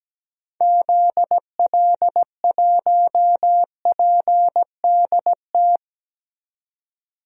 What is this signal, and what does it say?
Morse code, heard as a keyed tone, about 17 wpm, 700 Hz. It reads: ZL1PDT
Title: DXCC-100 Call Signs - 6 Characters - 17wpm